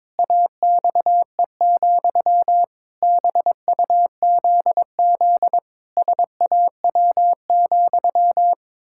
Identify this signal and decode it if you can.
Morse code, heard as a keyed tone, about 22 wpm, 700 Hz. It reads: AXE, BUZZ SAW,
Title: Pangram Challenge v2 - 22wpm